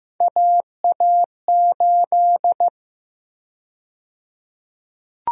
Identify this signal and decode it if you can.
Morse code, heard as a keyed tone, about 15 wpm, 700 Hz. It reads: AA8
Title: USA Call Sign Prefixes - Repeated 2x - 15wpm